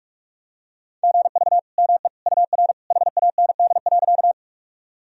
Morse code, heard as a keyed tone, about 46 wpm, 700 Hz. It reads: GV ME UR HAND.
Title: Sentences from Top 200 Words and Abbreviations 45wpm